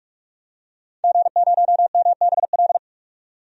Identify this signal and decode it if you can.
Morse code, heard as a keyed tone, about 46 wpm, 700 Hz. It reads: G0MBL
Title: DXCC-100 Call Signs - 5 Characters - 45wpm